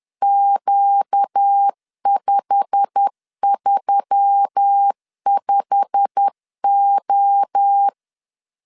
Morse code, heard as a keyed tone, about 10 wpm, 800 Hz. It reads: Q535O
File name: LEZIONI / 8 YJ45 / 8 YJ45 10wpm 2549